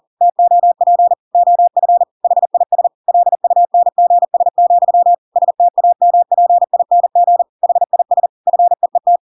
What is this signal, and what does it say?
Morse code, heard as a keyed tone, about 40 wpm, 700 Hz. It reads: TOP OF HIS LUNGS, STAMPING HIS FEET
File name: Pinocchio / Pinocchio ch5 40 40 ik1hge